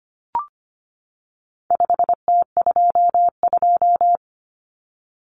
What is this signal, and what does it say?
Morse code, heard as a keyed tone, about 25 wpm, 700 Hz. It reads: 5T22
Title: Common Cut Numbers - 4-Digit 25wpm